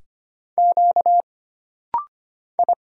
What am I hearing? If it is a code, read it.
Morse code, heard as a keyed tone, about 25 wpm, 700 Hz. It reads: Q I
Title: Single Letter-Number - Rapid-Fire 25wpm